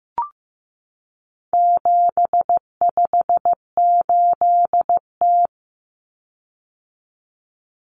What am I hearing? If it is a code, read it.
Morse code, heard as a keyed tone, about 15 wpm, 700 Hz. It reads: 758T
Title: Common Cut Numbers - 4-Digit 15wpm